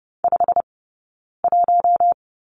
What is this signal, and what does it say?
Morse code, heard as a keyed tone, about 30 wpm, spaced as for 9 wpm, 690 Hz. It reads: 51
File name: Lesson 18d - Letter m - Two Characters 30wpm